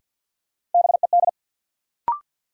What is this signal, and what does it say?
Morse code, heard as a keyed tone, about 49 wpm, 700 Hz. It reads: BED
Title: Sentences from Top 500 Words - New Word Drill - 50wpm